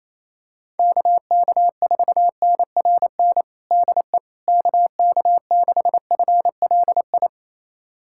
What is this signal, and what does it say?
Morse code, heard as a keyed tone, about 28 wpm, 700 Hz. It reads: KK4NRN DE KK6FLI
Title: USA Callsign DE Callsign - 6 Characters - 28wpm